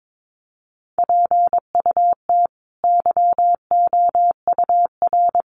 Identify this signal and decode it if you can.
Morse code, heard as a keyed tone, about 22 wpm, 700 Hz. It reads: PUT YOUR
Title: Sentences from Top 1100 Words 22wpm